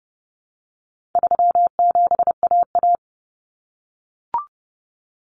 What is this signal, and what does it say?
Morse code, heard as a keyed tone, about 30 wpm, 700 Hz. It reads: 37AA
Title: Common Cut Numbers - 4-Digit 30wpm